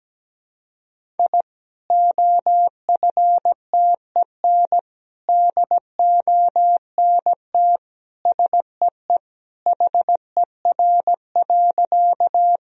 Morse code, heard as a keyed tone, about 17 wpm, 700 Hz. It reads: I OFTEN DONT SEE HER.
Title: Sentences from Top 1000 Words - Review Part 1 - 17wpm